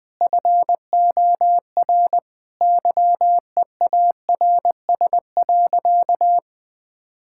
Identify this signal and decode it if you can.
Morse code, heard as a keyed tone, about 20 wpm, 700 Hz. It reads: FOR YEARS.
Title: Sentences with 6-Word Phrases - 20wpm